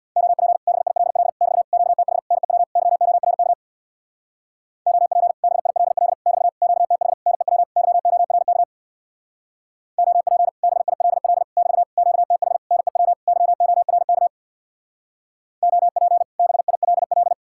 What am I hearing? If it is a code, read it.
Morse code, heard as a keyed tone, about 62 wpm, 700 Hz. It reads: OP BILL <BT> QTH NEW YORK OP BILL <BT> QTH NEW YORK OP BILL <BT> QTH NEW YORK OP BILL
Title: Speed-Racing Common QSO Phrases - 50wpm